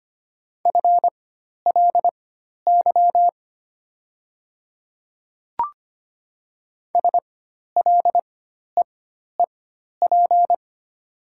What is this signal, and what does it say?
Morse code, heard as a keyed tone, about 25 wpm, spaced as for 12 wpm, 690 Hz. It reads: FLY SLEEP
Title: 4x Character Spacing - Top 500 Words - 25wpm